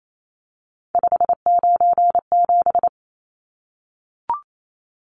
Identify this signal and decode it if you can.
Morse code, heard as a keyed tone, about 28 wpm, 700 Hz. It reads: 597
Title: Numbers - 3-Digit 28wpm